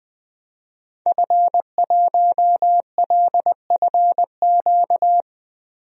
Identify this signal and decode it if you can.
Morse code, heard as a keyed tone, about 20 wpm, 700 Hz. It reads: F1LFQ
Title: DXCC-100 Call Signs - 5 Characters - Repeated 3x - 20wpm